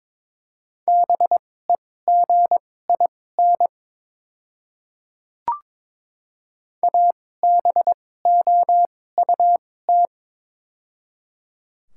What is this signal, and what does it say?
Morse code, heard as a keyed tone, about 22 wpm, spaced as for 16 wpm, 700 Hz. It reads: BEGIN ABOUT
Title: 2x Character Spacing - Top 300 Words - 22wpm